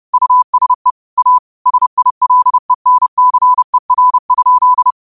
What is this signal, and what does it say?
Morse code, heard as a keyed tone, about 30 wpm, 1000 Hz. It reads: USE A SILENCER?